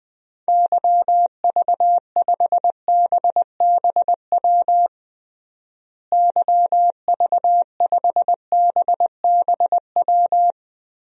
Morse code, heard as a keyed tone, about 21 wpm, 700 Hz. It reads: YV5BBW YV5BBW
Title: DXCC-100 Call Signs - 6 Characters - Repeated 3x - 20wpm